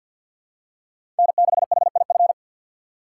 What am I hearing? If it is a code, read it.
Morse code, heard as a keyed tone, about 47 wpm, 700 Hz. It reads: N6HIF
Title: USA Call Signs - 5 Characters - 4x Character Spacing - 50wpm